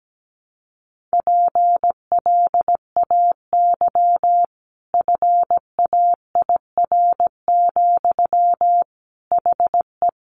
Morse code, heard as a keyed tone, about 17 wpm, 700 Hz. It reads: PLAY FAIR, HE
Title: Sentences from Top 1300 Words 17wpm